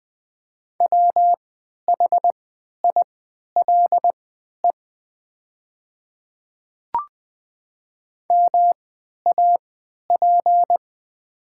Morse code, heard as a keyed tone, about 20 wpm, spaced as for 11 wpm, 700 Hz. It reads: WHILE MAP
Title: 3x Character Spacing - Top 500 Words - 20wpm